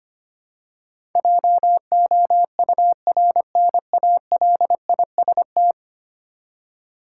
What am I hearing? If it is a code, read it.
Morse code, heard as a keyed tone, about 25 wpm, 700 Hz. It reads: JOURNALIST
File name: Ten Letter Words - 25wpm